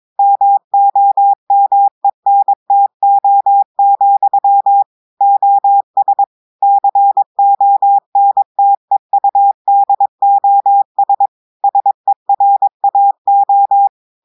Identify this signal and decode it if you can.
Morse code, heard as a keyed tone, about 22 wpm, 800 Hz. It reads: MOMENTO, OS CONTEUDOS SERAO
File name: audios / portugais / portugais texte en clair 2 vitesse 22 mots minutes